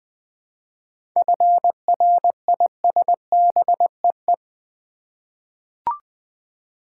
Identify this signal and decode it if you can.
Morse code, heard as a keyed tone, about 21 wpm, 700 Hz. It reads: FRISBEE
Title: Easy Words with Hints - 20wpm